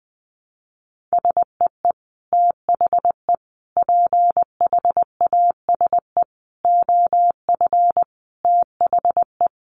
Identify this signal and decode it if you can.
Morse code, heard as a keyed tone, about 20 wpm, 700 Hz. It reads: SEE THE PHASE OF THE
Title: Sentences from Top 1200 Words 20wpm